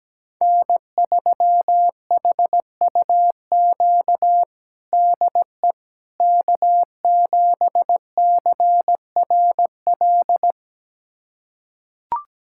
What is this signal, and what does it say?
Morse code, heard as a keyed tone, about 17 wpm, 700 Hz. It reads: N3HUQ DE K7CRL
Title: USA Callsign DE Callsign - 5 Characters - 17wpm